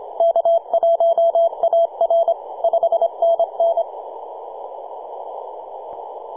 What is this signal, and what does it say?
Morse code, heard as a keyed tone, about 25 wpm, 670 Hz. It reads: K1AR 5NN
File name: posts / 2015-sept-26-2040z-v73a-21-025-cw